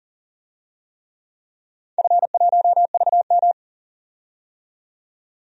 Morse code, heard as a keyed tone, about 40 wpm, 700 Hz. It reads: F1VM
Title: DXCC-100 Call Signs - 4 Characters - 40wpm